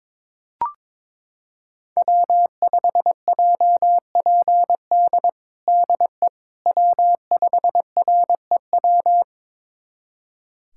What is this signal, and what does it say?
Morse code, heard as a keyed tone, about 22 wpm, 700 Hz. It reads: W5JPD DE W5REW
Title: USA Callsign DE Callsign - 5 Characters - 22wpm